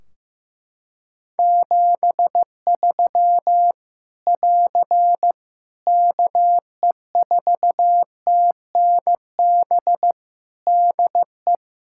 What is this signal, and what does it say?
Morse code, heard as a keyed tone, about 15 wpm, 700 Hz. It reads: 73 <AR> KE4TNB DE
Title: Practice QSO - Exchange 4 - Ending - 15wpm